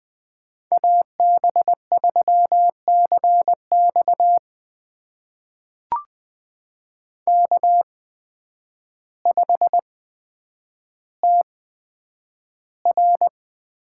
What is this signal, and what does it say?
Morse code, heard as a keyed tone, about 20 wpm, spaced as for 9 wpm, 700 Hz. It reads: AB3CX K5TR
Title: USA Call Signs - 8x Character Spacing - 20wpm